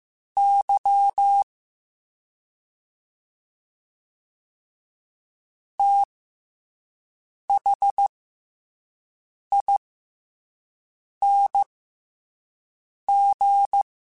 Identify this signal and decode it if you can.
Morse code, heard as a keyed tone, about 15 wpm, spaced as for 5 wpm, 800 Hz. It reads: Y THING